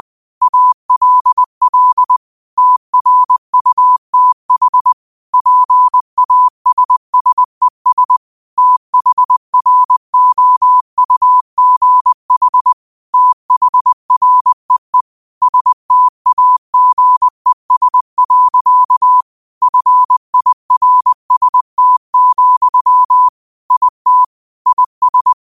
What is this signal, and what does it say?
Morse code, heard as a keyed tone, about 20 wpm, 1000 Hz. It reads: ALL TRUTH PASSES THROUGH THREE STAGES. FIRST, IT IS
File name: qotdfiles / 20wpm